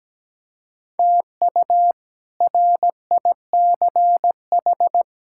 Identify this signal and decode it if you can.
Morse code, heard as a keyed tone, about 17 wpm, 700 Hz. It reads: TU RICH
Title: K1USN SST QSO Practice - 17wpm